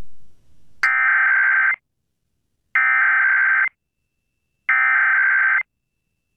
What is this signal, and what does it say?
Emergency Alert System (SAME) tones (header bursts), heard as harsh buzzing bursts. It is heard from 0.8 to 5.6 s.